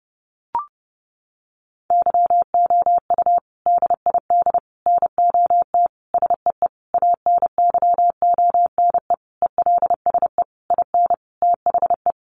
Morse code, heard as a keyed tone, about 30 wpm, 700 Hz. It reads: YOU DID NOT SEE ANYONE ELSE IN THE
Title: Sentences from Top 900 Words 30wpm